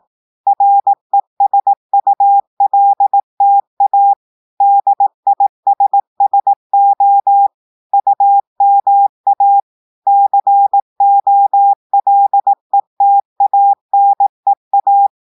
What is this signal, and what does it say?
Morse code, heard as a keyed tone, about 18 wpm, 800 Hz. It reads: RESULTA DISSO UMA COLETANEA
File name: audios / portugais / portugais texte en clair 1 vitesse 18 mots minutes